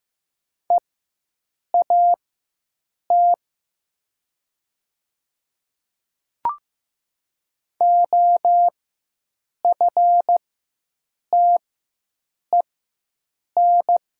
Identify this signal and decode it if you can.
Morse code, heard as a keyed tone, about 15 wpm, spaced as for 7 wpm, 700 Hz. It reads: EAT OFTEN
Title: 4x Character Spacing - Top 500 Words - 15wpm